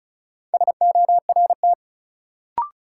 Morse code, heard as a keyed tone, about 34 wpm, 700 Hz. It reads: SORT
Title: Sentences from Top 1300 Words - New Word Drill Alt - 35wpm